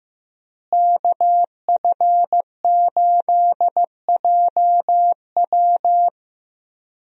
Morse code, heard as a keyed tone, about 15 wpm, 700 Hz. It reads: KF8JW
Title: USA Call Signs - 5 Characters - Repeated 2x - 15wpm